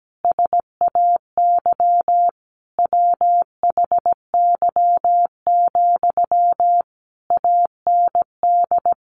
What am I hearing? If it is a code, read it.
Morse code, heard as a keyed tone, about 17 wpm, 700 Hz. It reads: SAY WHY, AND
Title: Sentences from Top 1300 Words 17wpm